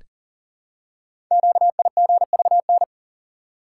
Morse code, heard as a keyed tone, about 41 wpm, 700 Hz. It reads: QIGVN
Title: Code Groups - 5 Letters - 40wpm